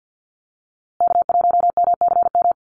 Morse code, heard as a keyed tone, about 50 wpm, 700 Hz. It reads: K1MCM
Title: USA Call Signs Repeated 3x 50wpm